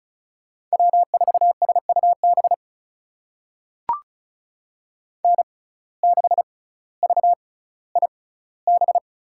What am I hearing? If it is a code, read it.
Morse code, heard as a keyed tone, about 34 wpm, spaced as for 18 wpm, 700 Hz. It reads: W4SUB N6VIB
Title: USA Call Signs - 6x Character Spacing - 35wpm